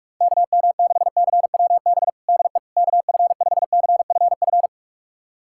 Morse code, heard as a keyed tone, about 42 wpm, 700 Hz. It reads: KM6CWB DE KF5CFF
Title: USA Callsign DE Callsign 45wpm